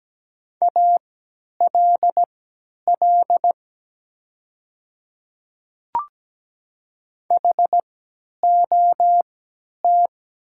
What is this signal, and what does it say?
Morse code, heard as a keyed tone, about 17 wpm, spaced as for 10 wpm, 700 Hz. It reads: ALL HOT
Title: 3x Character Spacing - Top 200 Words - 17wpm